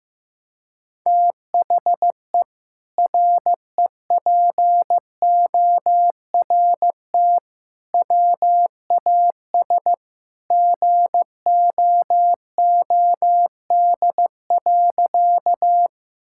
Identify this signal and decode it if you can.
Morse code, heard as a keyed tone, about 16 wpm, 700 Hz. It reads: THE REPORT WAS GOOD.
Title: Sentences from Top 1000 Words - Review Part 6 - 15wpm